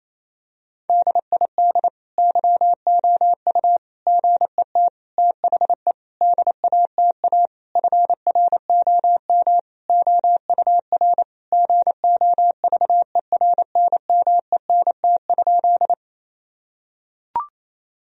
Morse code, heard as a keyed tone, about 28 wpm, 700 Hz. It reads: DID YOU GET THE DATA FROM OUR GOVERNMENT?
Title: Sentences from Top 1000 Words - Review Part 3 - 28wpm